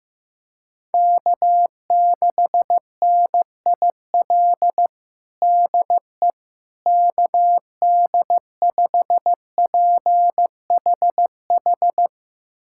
Morse code, heard as a keyed tone, about 15 wpm, 700 Hz. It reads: K6NIL DE KD5PHH
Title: USA Callsign DE Callsign 15wpm